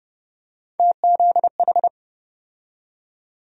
Morse code, heard as a keyed tone, about 30 wpm, 700 Hz. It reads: TZH
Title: USA Call Sign Postfixes - 3 Letters 30wpm